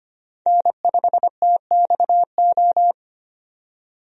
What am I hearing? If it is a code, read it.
Morse code, heard as a keyed tone, about 25 wpm, 700 Hz. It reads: N5TXO
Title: USA Call Signs - 5 Characters - 25wpm